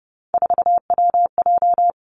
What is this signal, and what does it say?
Morse code, heard as a keyed tone, about 30 wpm, 700 Hz. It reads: 4WJ
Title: Three Challenging Letter-Number Combinations 30wpm